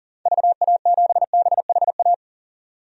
Morse code, heard as a keyed tone, about 40 wpm, 700 Hz. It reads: VA7BHA
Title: DXCC-100 Call Signs - 6 Characters - 40wpm